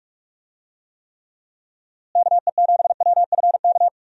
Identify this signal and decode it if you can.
Morse code, heard as a keyed tone, about 46 wpm, 700 Hz. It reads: KE7WFK
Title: USA Call Signs - 6 Characters - Repeated 2x - 45wpm